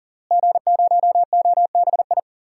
Morse code, heard as a keyed tone, about 40 wpm, 700 Hz. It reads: G0OBI
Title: DXCC-100 Call Signs - 5 Characters - Repeated 2x - 40wpm